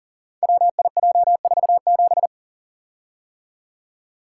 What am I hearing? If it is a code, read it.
Morse code, heard as a keyed tone, about 42 wpm, 700 Hz. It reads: WIJ47
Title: Code Groups - 5 Letter:Numbers - 40wpm